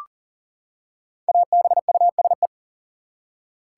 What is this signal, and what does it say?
Morse code, heard as a keyed tone, about 40 wpm, 700 Hz. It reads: ABUSE
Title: Five Letter Words - 40wpm